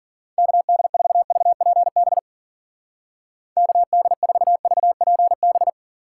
Morse code, heard as a keyed tone, about 50 wpm, 700 Hz. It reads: KD4VPB KD4VPB
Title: Speed Racing - USA 2x3 Call Signs 40wpm